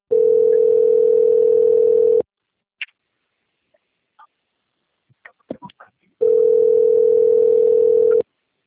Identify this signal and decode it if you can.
Ringback tone (North America), 440 + 480 Hz.